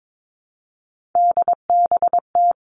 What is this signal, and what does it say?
Morse code, heard as a keyed tone, about 22 wpm, 700 Hz. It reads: DBT
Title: Three Challenging Letter-Number Combinations 22wpm